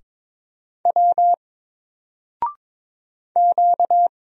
Morse code, heard as a keyed tone, about 22 wpm, 700 Hz. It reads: W Q
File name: Single-Letter-Number-22wpm